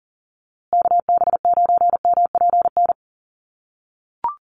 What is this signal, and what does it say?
Morse code, heard as a keyed tone, about 40 wpm, 700 Hz. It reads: KB9MPN
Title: USA Call Signs Repeated 3x 40wpm